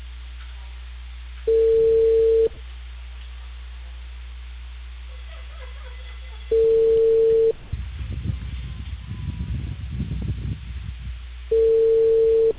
Telephone call progress tone (China, 450 Hz), heard as a telephone network's tone, about 450 Hz. Ringback tone.